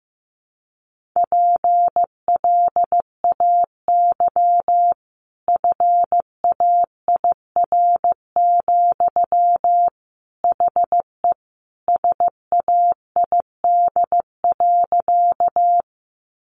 Morse code, heard as a keyed tone, about 15 wpm, 700 Hz. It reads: PLAY FAIR, HE SAID.
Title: Sentences from Top 1300 Words 15wpm